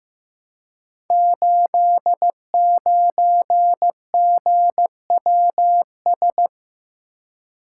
Morse code, heard as a keyed tone, about 15 wpm, 700 Hz. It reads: 89GWS
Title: Code Groups - 5 Letter:Numbers - 15wpm